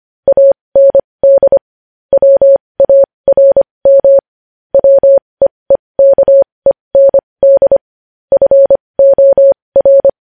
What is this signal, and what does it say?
Morse code, heard as a keyed tone, about 25 wpm, spaced as for 20 wpm, 550 Hz. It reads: AND WARM WEEKEND FOR
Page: news-headlines-in-morse-code-at-20-wpm